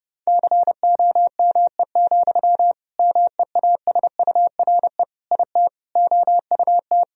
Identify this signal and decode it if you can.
Morse code, heard as a keyed tone, about 30 wpm, 700 Hz. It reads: COME, MEASURE IT OUT
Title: Sentences from Top 1000 Words - Review Part 2 - 30wpm